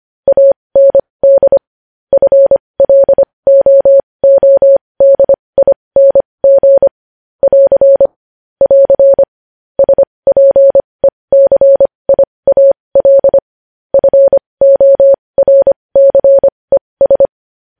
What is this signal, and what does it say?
Morse code, heard as a keyed tone, about 25 wpm, spaced as for 20 wpm, 550 Hz. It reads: AND FLOODING <AR> <AR> SPECIAL FORCES